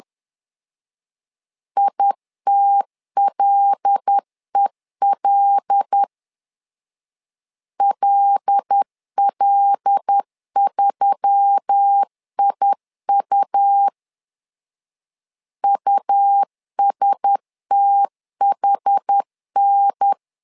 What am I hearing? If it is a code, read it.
Morse code, heard as a keyed tone, about 10 wpm, 800 Hz. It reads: ITLEL LL3IU USTHN
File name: LEZIONI / 5 LBVC / 5 L 10wpm 2259